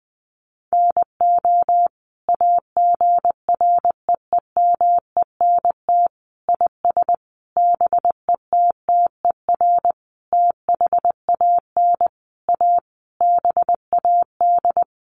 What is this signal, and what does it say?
Morse code, heard as a keyed tone, about 20 wpm, 700 Hz. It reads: NO AGREEMENT IS BETTER THAN A BAD
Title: Sentences from Top 1500 Words 20wpm